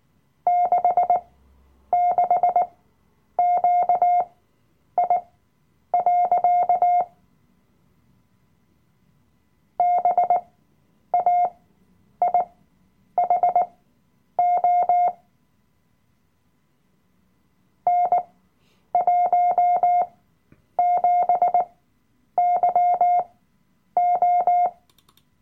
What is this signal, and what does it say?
Morse code, heard as a keyed tone, about 18 wpm, spaced as for 9 wpm, 710 Hz. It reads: 66QI. BAIHO N17YO
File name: Workshop Dateien / CW-Online-Kurs / Lektion 007